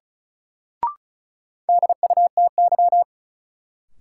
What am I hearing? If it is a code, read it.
Morse code, heard as a keyed tone, about 34 wpm, 700 Hz. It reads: DUTY
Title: Sentences from Top 1500 Words - New Word Drill - 35wpm